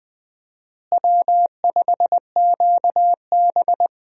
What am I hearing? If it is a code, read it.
Morse code, heard as a keyed tone, about 20 wpm, 700 Hz. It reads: W5QB
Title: USA Call Signs - 4 Characters - Repeated 3x - 20wpm